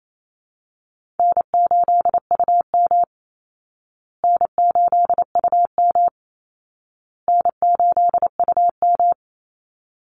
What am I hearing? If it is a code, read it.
Morse code, heard as a keyed tone, about 28 wpm, 700 Hz. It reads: N8UM N8UM N8UM
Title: Call Signs Repeated 2x - 28wpm